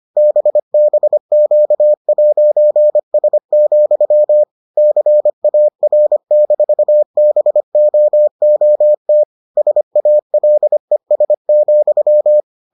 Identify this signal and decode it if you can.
Morse code, heard as a keyed tone, about 25 wpm, 600 Hz. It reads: BBQ'S, CAR-BOOT SALES,